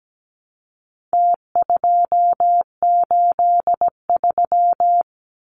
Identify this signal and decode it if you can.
Morse code, heard as a keyed tone, about 17 wpm, 700 Hz. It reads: T283
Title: Numbers - Contest Style - 4-Digit 17wpm